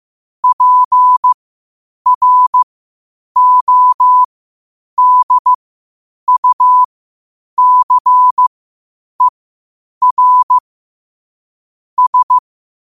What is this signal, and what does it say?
Morse code, heard as a keyed tone, about 15 wpm, spaced as for 8 wpm, 1000 Hz. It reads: PRODUCER S
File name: qotdfiles / 8wpm